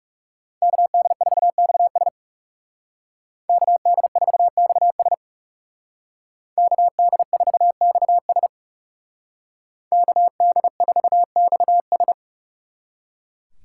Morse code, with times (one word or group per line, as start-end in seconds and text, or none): 0.6-2.1 KD4XS
3.5-5.1 KD4XS
6.6-8.5 KD4XS
9.9-12.1 KD4XS